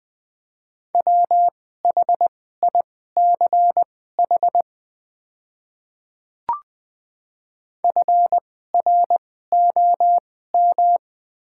Morse code, heard as a keyed tone, about 20 wpm, spaced as for 14 wpm, 700 Hz. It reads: WHICH FROM
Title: 2x Character Spacing - Top 100 Words - 20wpm